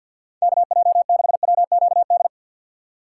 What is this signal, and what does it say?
Morse code, heard as a keyed tone, about 50 wpm, 700 Hz. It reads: KJ6WQD
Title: USA Call Signs - 2x Character Spacing - 50wpm